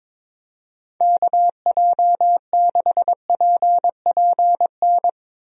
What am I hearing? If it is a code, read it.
Morse code, heard as a keyed tone, about 22 wpm, 700 Hz. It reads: KJ6PPN
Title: USA Call Signs - 6 Characters - Repeated 2x - 22wpm